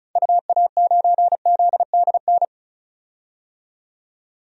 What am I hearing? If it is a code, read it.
Morse code, heard as a keyed tone, about 35 wpm, 700 Hz. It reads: UA9ZDN
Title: DXCC-100 Call Signs - 6 Characters - 35wpm